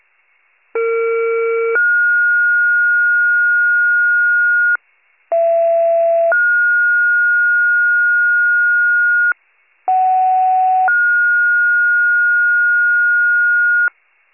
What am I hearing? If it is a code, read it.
Two-tone page, heard as a one-second tone, then a three-second tone, about 460 then 1430 Hz, about 670 then 1430 Hz, about 750 then 1430 Hz. It is heard from 0.7 to 13.9 s.